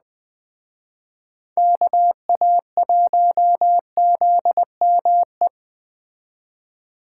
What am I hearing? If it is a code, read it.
Morse code, heard as a keyed tone, about 20 wpm, 700 Hz. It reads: KA1ZME
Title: Speed Racing - USA 2x3 Call Signs 17wpm